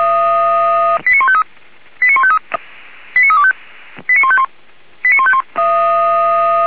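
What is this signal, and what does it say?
Five-tone selective call (ZVEI), heard as quick runs of five tones, tones of about 70 ms. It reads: 89153 89153 89325 89151 89151
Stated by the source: ZVEI five-tone selective call received on an SDR.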